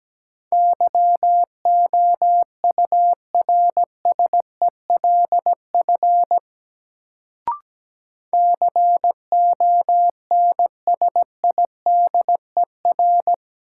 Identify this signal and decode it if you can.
Morse code, heard as a keyed tone, about 17 wpm, 700 Hz. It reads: YOURSELF CONSIDER
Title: Sentences from Top 600 Words - New Word Drill - 17wpm